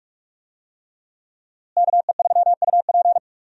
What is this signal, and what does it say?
Morse code, heard as a keyed tone, about 46 wpm, 700 Hz. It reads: KE3UP
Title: USA Call Signs - 5 Characters - Repeated 3x - 45wpm